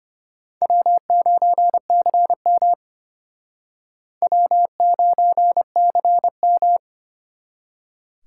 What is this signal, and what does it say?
Morse code, heard as a keyed tone, about 32 wpm, 700 Hz. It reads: W9CM W9CM